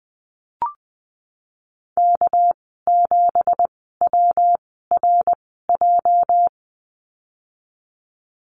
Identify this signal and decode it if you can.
Morse code, heard as a keyed tone, about 20 wpm, 700 Hz. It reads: K 7 W R J
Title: 207 Lesson 33g - Letter j - Call Signs 20wpm